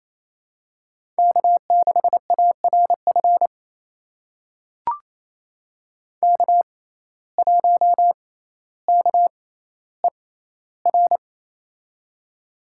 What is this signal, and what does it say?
Morse code, heard as a keyed tone, about 28 wpm, spaced as for 14 wpm, 700 Hz. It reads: K6ARF K1KER
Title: USA Call Signs - 5 Characters - 6x Character Spacing - 28wpm